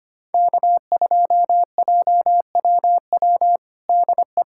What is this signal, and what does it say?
Morse code, heard as a keyed tone, about 25 wpm, 700 Hz. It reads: K2JWW DE
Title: USA Callsign DE Callsign 25wpm